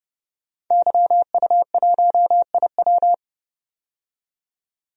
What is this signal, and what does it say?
Morse code, heard as a keyed tone, about 32 wpm, 700 Hz. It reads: YU1IW
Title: DXCC-100 Call Signs - 5 Characters - 30wpm